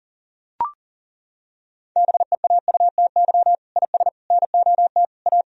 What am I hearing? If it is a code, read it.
Morse code, heard as a keyed tone, about 40 wpm, 700 Hz. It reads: BEAUTY IS NOT A
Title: Sentences from Top 1000 Words - Review Part 4 - 40wpm